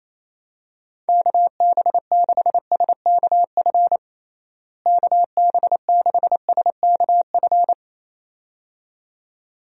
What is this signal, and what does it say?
Morse code, heard as a keyed tone, about 28 wpm, 700 Hz. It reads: KB6SKF KB6SKF
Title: USA Call Signs - 6 Characters - Repeated 2x - 28wpm